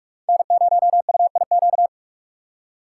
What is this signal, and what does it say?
Morse code, heard as a keyed tone, about 44 wpm, 700 Hz. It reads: N0UIQ
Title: USA Call Signs - 5 Characters - 3x Character Spacing - 45wpm